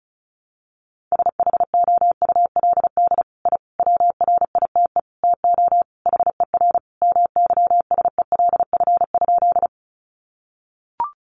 Morse code, heard as a keyed tone, about 34 wpm, 700 Hz. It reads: SHOULD I WRITE TO HER MYSELF?
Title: Sentences from Top 1300 Words 35wpm